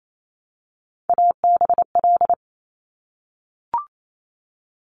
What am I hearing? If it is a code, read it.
Morse code, heard as a keyed tone, about 28 wpm, 700 Hz. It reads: ABL
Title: Top Three Letter Combinations in Words 28wpm